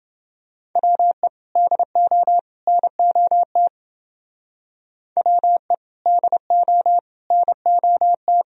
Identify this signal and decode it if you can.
Morse code, heard as a keyed tone, about 29 wpm, 700 Hz. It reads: WE DO NOT WE DO NOT